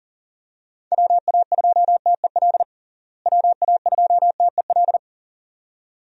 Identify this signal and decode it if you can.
Morse code, heard as a keyed tone, about 40 wpm, 700 Hz. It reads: WA2TEL WA2TEL
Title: USA Call Signs - 6 Characters - Repeated 2x - 40wpm